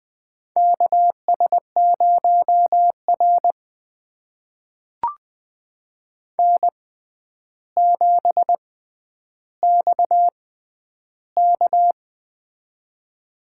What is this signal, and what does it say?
Morse code, heard as a keyed tone, about 20 wpm, spaced as for 10 wpm, 700 Hz. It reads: KS0R N7XK
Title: USA Call Signs - 6x Character Spacing - 20wpm